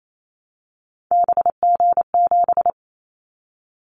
Morse code, heard as a keyed tone, about 28 wpm, 700 Hz. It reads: BG7
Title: Callsign Prefixes for Top 100 DXCC - 28wpm